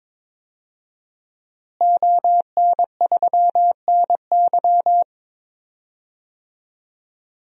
Morse code, heard as a keyed tone, about 22 wpm, 700 Hz. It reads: ON3NY
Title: DXCC-100 Call Signs - 5 Characters - Repeated 3x - 22wpm